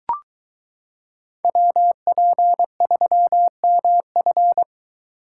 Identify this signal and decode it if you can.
Morse code, heard as a keyed tone, about 22 wpm, 700 Hz. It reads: WP3MF